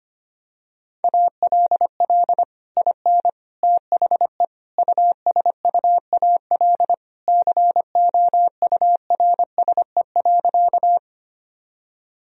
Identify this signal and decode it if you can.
Morse code, heard as a keyed tone, about 25 wpm, 700 Hz. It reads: ALL IN THE USUAL COURSE.
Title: Sentences from Top 1000 Words - Review Part 2 - 25wpm